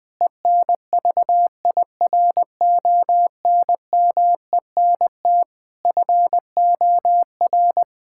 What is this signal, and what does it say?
Morse code, heard as a keyed tone, about 20 wpm, 700 Hz. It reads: ENVIRONMENT FOR
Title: Difficult Sentences with Hints - 20wpm